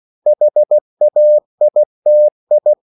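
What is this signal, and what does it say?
Morse code, heard as a keyed tone, about 16 wpm, 600 Hz. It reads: HAITI
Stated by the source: CW practice of national capitals at 16 words per minute, from ON6ZQ.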